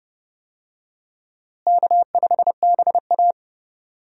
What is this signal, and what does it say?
Morse code, heard as a keyed tone, about 30 wpm, 700 Hz. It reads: K5BA
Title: USA Call Signs - 4 Characters - Repeated 2x - 30wpm